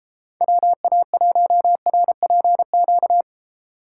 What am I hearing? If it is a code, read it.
Morse code, heard as a keyed tone, about 33 wpm, 700 Hz. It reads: WA1RPQ